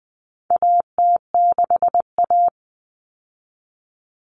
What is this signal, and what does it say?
Morse code, heard as a keyed tone, about 20 wpm, 700 Hz. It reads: AT6A
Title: Common Cut Numbers - 4-Digit 20wpm